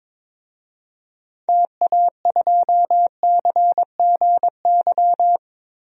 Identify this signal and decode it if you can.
Morse code, heard as a keyed tone, about 22 wpm, 700 Hz. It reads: TA2CGY
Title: DXCC-100 Call Signs - 6 Characters - Repeated 2x - 22wpm